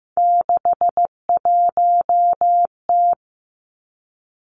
Morse code, heard as a keyed tone, about 16 wpm, 700 Hz. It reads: 61T
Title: Three Challenging Letter-Number Combinations 15wpm